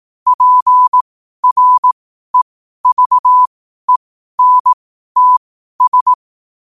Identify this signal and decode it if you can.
Morse code, heard as a keyed tone, about 18 wpm, spaced as for 12 wpm, 1000 Hz. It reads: PREVENTS